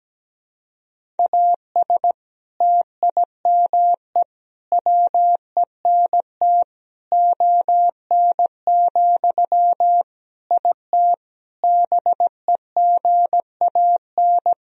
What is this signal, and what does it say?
Morse code, heard as a keyed tone, about 17 wpm, 700 Hz. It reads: AS TIME WENT ON, IT BEGAN
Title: Sentences from Top 1000 Words - Review Part 1 - 17wpm